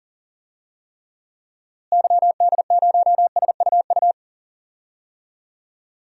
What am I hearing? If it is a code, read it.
Morse code, heard as a keyed tone, about 40 wpm, 700 Hz. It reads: YD0SUU
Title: DXCC-100 Call Signs - 6 Characters - Repeated 2x - 40wpm